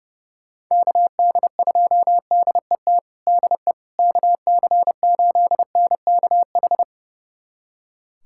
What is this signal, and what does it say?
Morse code, heard as a keyed tone, about 29 wpm, 700 Hz. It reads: KD2DET DE KC8NKH
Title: USA Callsign DE Callsign - 6 Characters - 30wpm